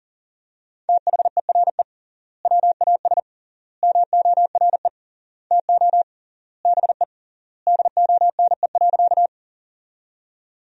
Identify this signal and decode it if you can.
Morse code, heard as a keyed tone, about 40 wpm, 700 Hz. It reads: THERE WAS MORE TO BE DONE.
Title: 3x Word-Spacing - Sentences from Top 500 Words - 40wpm